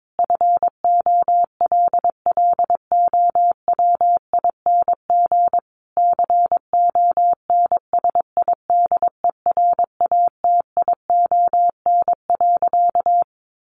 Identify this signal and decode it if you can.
Morse code, heard as a keyed tone, about 22 wpm, 700 Hz. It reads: FOLLOWING CONSIDERATION.
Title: Sentences from Top 1500 Words 22wpm